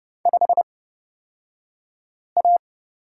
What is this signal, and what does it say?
Morse code, heard as a keyed tone, about 30 wpm, spaced as for 5 wpm, 700 Hz. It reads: 5A
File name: Lesson 37d - Letter x - Two Characters 30wpm